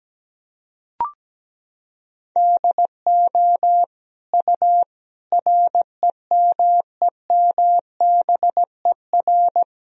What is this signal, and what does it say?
Morse code, heard as a keyed tone, about 17 wpm, 700 Hz. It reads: DO U REMEMBER
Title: Sentences from Top 400 Words and Abbreviations 17wpm